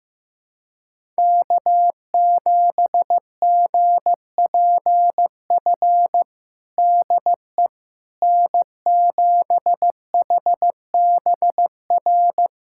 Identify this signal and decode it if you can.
Morse code, heard as a keyed tone, about 15 wpm, 700 Hz. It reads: K7GPF DE N7HBR
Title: USA Callsign DE Callsign - 5 Characters - 15wpm